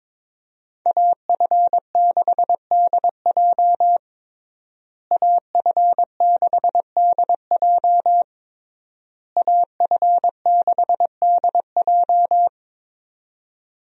Morse code, heard as a keyed tone, about 22 wpm, 700 Hz. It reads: AF6DJ AF6DJ AF6DJ
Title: USA Call Signs - 5 Characters - Repeated 3x - 22wpm